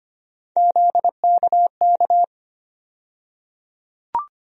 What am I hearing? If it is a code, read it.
Morse code, heard as a keyed tone, about 25 wpm, 700 Hz. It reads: ZKK
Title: USA Call Sign Postfixes - 3 Letters 25wpm